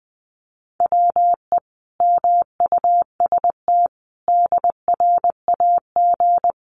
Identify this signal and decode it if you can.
Morse code, heard as a keyed tone, about 20 wpm, 700 Hz. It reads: WE MUST DRAG
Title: Sentences from Top 1500 Words 20wpm